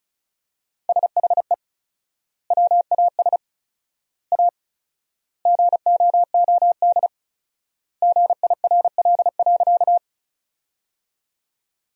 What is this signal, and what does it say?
Morse code, heard as a keyed tone, about 34 wpm, 700 Hz. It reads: SHE WAS A GOOD GIRL.
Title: 4x Word-Spacing - Sentences from Top 400 Words - 35wpm